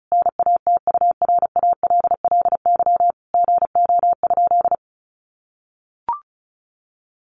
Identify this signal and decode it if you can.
Morse code, heard as a keyed tone, about 34 wpm, 700 Hz. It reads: NATURALLY GO?
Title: Sentences from Top 1300 Words 35wpm